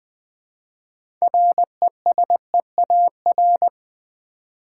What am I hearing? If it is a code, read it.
Morse code, heard as a keyed tone, about 20 wpm, 700 Hz. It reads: RESEAR
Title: Build A Word - 2 Letters At A Time - 8-Letter Words - 20wpm